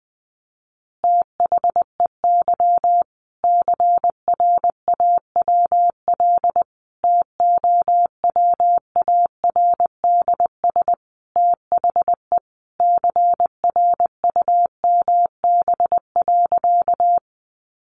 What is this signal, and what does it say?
Morse code, heard as a keyed tone, about 20 wpm, 700 Hz. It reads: THEY CRAWL TOWARDS THE CRUMB.
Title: Additional Easy Phrases 20wpm